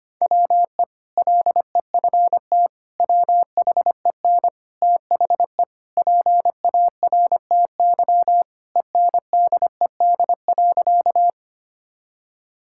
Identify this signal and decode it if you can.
Morse code, heard as a keyed tone, about 25 wpm, 700 Hz. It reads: WE LEFT WHEN THE PARTY ENDED.
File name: practice / 25wpm / Sentences with 2-Word Phrases - 25wpm